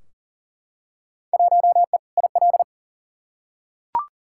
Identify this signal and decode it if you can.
Morse code, heard as a keyed tone, about 37 wpm, 700 Hz. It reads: 1E IL
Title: Field Day Exchanges - 40wpm